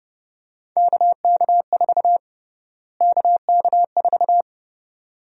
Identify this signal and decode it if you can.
Morse code, heard as a keyed tone, about 30 wpm, 700 Hz. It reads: KK4 KK4
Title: USA Call Sign Prefixes - Repeated 2x - 30wpm